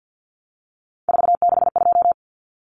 Morse code, heard as a keyed tone, about 50 wpm, 700 Hz. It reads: 462
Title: Common Cut Numbers - 3-Digit 50wpm